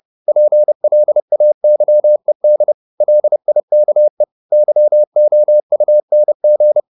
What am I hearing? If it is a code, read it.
Morse code, heard as a keyed tone, about 30 wpm, 600 Hz. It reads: PLAYED LIKE YOUNG
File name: the old man and the sea / the old man and the sea 1 30wpm